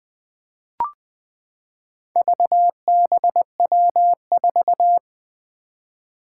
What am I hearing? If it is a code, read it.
Morse code, heard as a keyed tone, about 20 wpm, 700 Hz. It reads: VBW4
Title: Four Challenging Letter-Number Combinations 20wpm